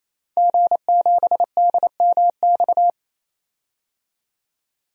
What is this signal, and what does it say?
Morse code, heard as a keyed tone, about 28 wpm, 700 Hz. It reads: G7DMX
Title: DXCC-100 Call Signs - 5 Characters - 28wpm